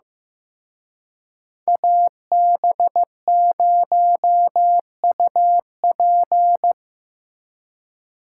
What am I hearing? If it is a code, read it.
Morse code, heard as a keyed tone, about 17 wpm, 700 Hz. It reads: AB0UP
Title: Speed Racing - USA Call Signs 15wpm